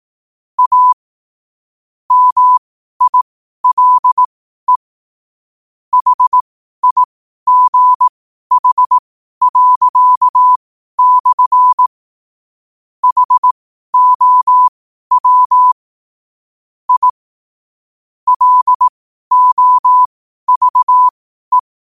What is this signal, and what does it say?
Morse code, heard as a keyed tone, about 18 wpm, spaced as for 12 wpm, 1000 Hz. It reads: A MILE HIGH./ HOW I LOVE
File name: qotdfiles / 12wpm